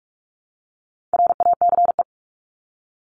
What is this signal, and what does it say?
Morse code, heard as a keyed tone, about 44 wpm, 700 Hz. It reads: RACE
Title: Top 1501-1750 Words 45wpm